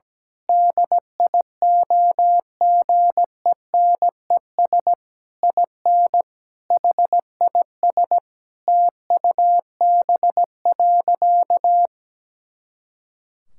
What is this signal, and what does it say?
Morse code, heard as a keyed tone, about 17 wpm, 700 Hz. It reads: DIOGENES IN HIS TUB.
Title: Pangram Challenge v2 - 17wpm